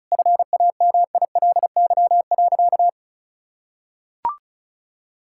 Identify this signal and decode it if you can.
Morse code, heard as a keyed tone, about 34 wpm, 700 Hz. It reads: FAMILY.
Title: Sentences from Top 1000 Words - Review Part 6 - 35wpm